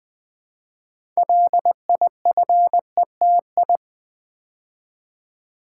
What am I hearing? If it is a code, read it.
Morse code, heard as a keyed tone, about 20 wpm, 700 Hz. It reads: LIFETI
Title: Build A Word - 3 Letters At A Time - 8-Letter Words - 20wpm